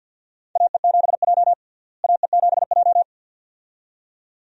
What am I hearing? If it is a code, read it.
Morse code, heard as a keyed tone, about 52 wpm, 700 Hz. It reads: AE7J AE7J
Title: USA Call Signs - 4 Characters - Repeated 2x - 50wpm